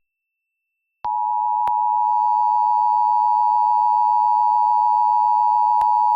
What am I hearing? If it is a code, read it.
Emergency Alert System (SAME) tones (the 853 and 960 Hz attention tone). It is heard from 1.0 to 5.8 s.
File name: wrct-log hi- 19